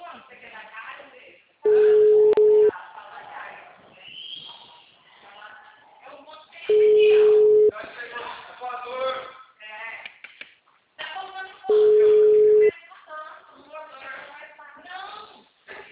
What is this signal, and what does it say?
Ringback tone (Europe, 425 Hz), with speech.